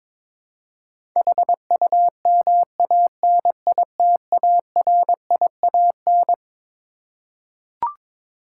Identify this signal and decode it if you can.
Morse code, heard as a keyed tone, about 22 wpm, 700 Hz. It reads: HUMANITARIAN
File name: Twelve Letter Words - 22wpm